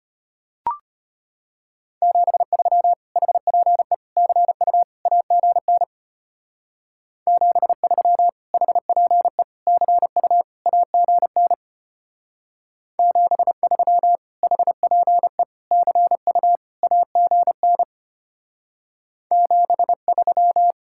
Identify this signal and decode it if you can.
Morse code, heard as a keyed tone, about 30 wpm, 700 Hz. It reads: 73 HPE CU AGN 73 HPE CU AGN 73 HPE CU AGN 73
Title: Speed-Racing Common QSO Phrases - 25wpm